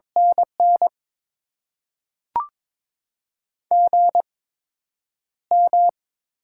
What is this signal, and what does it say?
Morse code, heard as a keyed tone, about 22 wpm, spaced as for 9 wpm, 700 Hz. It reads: NN GM
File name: Top 20 QSO Elements - 8x Character Spacing - 22wpm